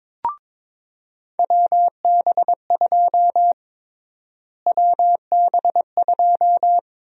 Morse code, heard as a keyed tone, about 22 wpm, 700 Hz. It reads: WB2 WB2
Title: USA Call Sign Prefixes - Repeated 2x - 22wpm